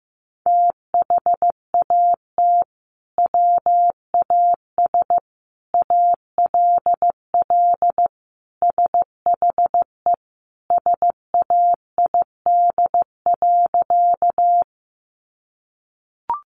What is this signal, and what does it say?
Morse code, heard as a keyed tone, about 15 wpm, 700 Hz. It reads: THAT WAS ALL SHE SAID.
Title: Sentences from Top 100 Words 15wpm